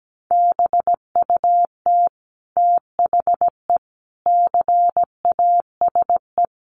Morse code, heard as a keyed tone, about 17 wpm, 700 Hz. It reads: BUT THE CASE
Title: Sentences from Top 700 Words 17wpm